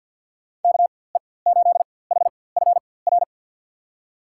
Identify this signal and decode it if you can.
Morse code, heard as a keyed tone, about 53 wpm, spaced as for 24 wpm, 700 Hz. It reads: KE8HFR
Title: USA Call Signs - 4x Character Spacing - 50wpm